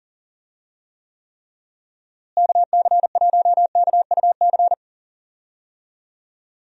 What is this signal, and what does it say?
Morse code, heard as a keyed tone, about 40 wpm, 700 Hz. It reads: KC1KUC
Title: USA Call Signs - 6 Characters - Repeated 3x - 40wpm